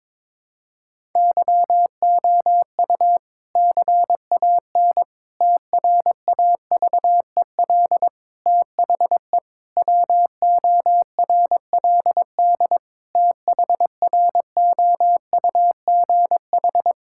Morse code, heard as a keyed tone, about 22 wpm, 700 Hz. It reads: YOU CAN TRAVEL THE WORLD THROUGH
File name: Difficult Sentences with Hints - 22wpm